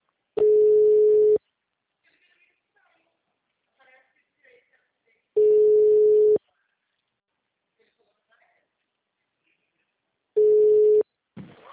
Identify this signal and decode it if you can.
Ringback tone (Europe, 425 Hz), with speech.